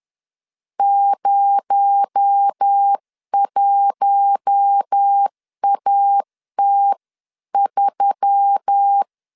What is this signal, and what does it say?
Morse code, heard as a keyed tone, about 10 wpm, 800 Hz. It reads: 01AT3
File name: CW / DET / 1 6 1500C S C 10wpm 4252